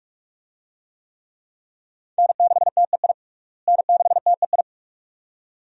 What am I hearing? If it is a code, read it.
Morse code, heard as a keyed tone, about 46 wpm, 700 Hz. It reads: N6TEI N6TEI
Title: USA Call Signs - 5 Characters - Repeated 2x - 45wpm